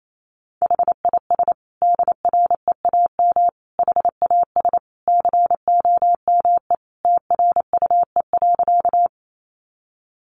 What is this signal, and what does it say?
Morse code, heard as a keyed tone, about 28 wpm, 700 Hz. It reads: HIS DREAM HAS COME TRUE.
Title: Sentences from Top 1500 Words 28wpm